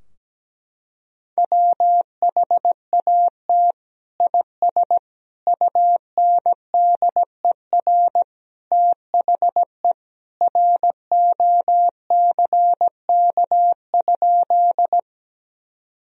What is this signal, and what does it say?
Morse code, heard as a keyed tone, about 17 wpm, 700 Hz. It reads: WHAT IS UNDER THE ROCK?
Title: Sentences from Top 1000 Words - Review Part 2 - 17wpm